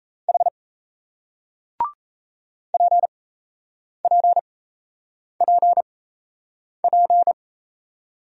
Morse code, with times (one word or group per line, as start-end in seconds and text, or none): 0.3-0.5 H
2.7-7.3 PPPP